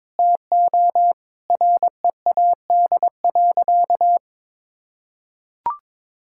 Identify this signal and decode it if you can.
Morse code, heard as a keyed tone, about 22 wpm, 700 Hz. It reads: TO READ.
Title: Sentences from Top 1000 Words - Review Part 5 - 22wpm